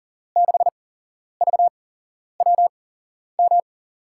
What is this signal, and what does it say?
Morse code, heard as a keyed tone, about 40 wpm, spaced as for 11 wpm, 700 Hz. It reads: 6VWM
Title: USA Call Signs - 6 Characters - 8x Character Spacing - 40wpm